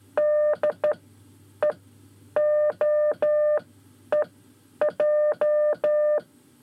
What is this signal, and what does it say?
Morse code, heard as a keyed tone, about 13 wpm, spaced as for 9 wpm, 600 Hz. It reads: DEOEJ
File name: 5 Lezione CW 1